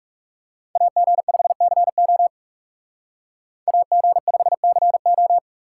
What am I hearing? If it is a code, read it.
Morse code, heard as a keyed tone, about 38 wpm, 700 Hz. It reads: AG5CO AG5CO